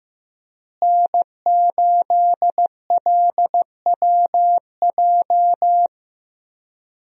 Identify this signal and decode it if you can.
Morse code, heard as a keyed tone, about 15 wpm, 700 Hz. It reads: N8LWJ
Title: USA Call Signs - 5 Characters - Repeated 3x - 15wpm